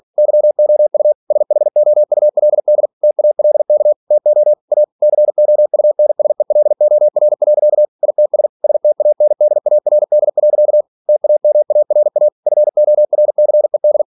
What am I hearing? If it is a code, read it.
Morse code, heard as a keyed tone, about 46 wpm, 600 Hz. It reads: YOU SHOULD TALK TO A COUNSELOR. ITS STANDARD. TAMARA FORCED